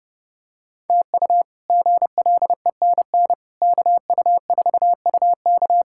Morse code, heard as a keyed tone, about 30 wpm, 700 Hz. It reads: TU GLENN KU4UK
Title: K1USN SST QSO Practice - 30wpm